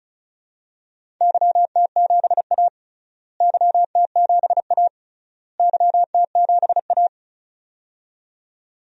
Morse code, heard as a keyed tone, about 34 wpm, 700 Hz. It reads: YT7A YT7A YT7A
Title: DXCC-100 Call Signs - 4 Characters - Repeated 3x - 35wpm